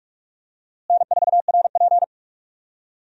Morse code, heard as a keyed tone, about 45 wpm, 700 Hz. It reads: N4RP
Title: USA Call Signs - 4 Characters - Repeated 3x - 45wpm